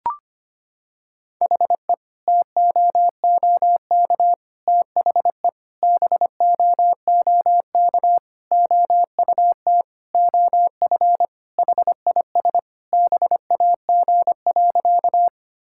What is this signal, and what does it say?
Morse code, heard as a keyed tone, about 25 wpm, 700 Hz. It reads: HE TOOK THE BOOK OUT OF HIS BAG.
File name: practice / 25wpm / Sentences with 3-Word Phrases - 25wpm